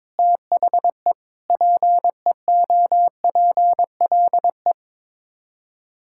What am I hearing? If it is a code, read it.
Morse code, heard as a keyed tone, about 23 wpm, 700 Hz. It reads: THE PEOPLE
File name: Sets of 9 Words - 22wpm